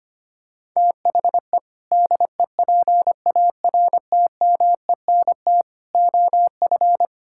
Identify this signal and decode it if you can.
Morse code, heard as a keyed tone, about 25 wpm, 700 Hz. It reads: THE DEPARTMENT OF
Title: Sentences from Top 1000 Words - Review Part 5 - 25wpm